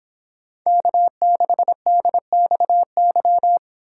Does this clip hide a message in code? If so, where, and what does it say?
Yes: Morse code from 0.7 to 3.6 s, heard as a keyed tone, about 26 wpm, 700 Hz. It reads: K6DXY